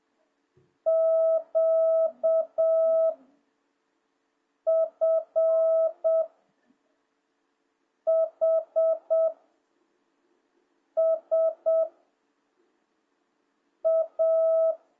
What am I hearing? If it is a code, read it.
Morse code, heard as a keyed tone, about 7 wpm, 660 Hz. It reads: Q F H S A